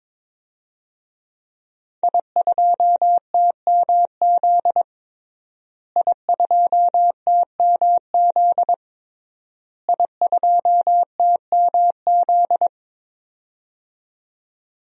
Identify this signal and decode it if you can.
Morse code, heard as a keyed tone, about 22 wpm, 700 Hz. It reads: I2TMZ I2TMZ I2TMZ
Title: DXCC-100 Call Signs - 5 Characters - Repeated 3x - 22wpm